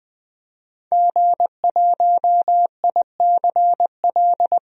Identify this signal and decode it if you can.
Morse code, heard as a keyed tone, about 21 wpm, 700 Hz. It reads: G1ICL
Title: DXCC-100 Call Signs - 5 Characters - 20wpm